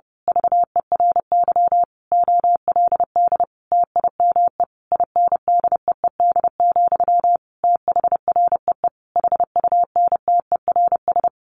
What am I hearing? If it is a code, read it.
Morse code, heard as a keyed tone, about 30 wpm, 700 Hz. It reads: VERY OLD TIME INDEED, THREE HUNTERS
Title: A Sound Of Thunder 30wpm